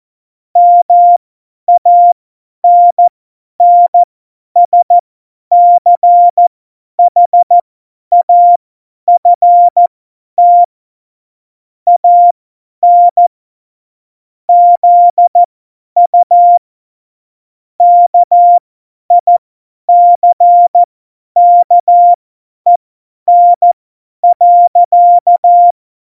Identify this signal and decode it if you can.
Morse code, heard as a keyed tone, about 14 wpm, spaced as for 10 wpm, 700 Hz. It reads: MANNSCHAFT AN ZU KICKEN.
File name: Morsekurs / Witze / Witz 22. Dezember 2020, 50 F70